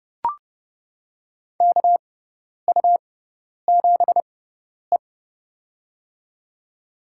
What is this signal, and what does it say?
Morse code, heard as a keyed tone, about 30 wpm, spaced as for 10 wpm, 700 Hz. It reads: KU7E
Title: USA Call Signs - 4 Characters - 6x Character Spacing - 30wpm